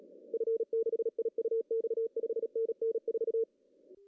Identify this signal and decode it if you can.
Morse code, heard as a keyed tone, about 37 wpm, 470 Hz. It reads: F6IUX 5NN 4
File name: CW / cqww-cw-k5tr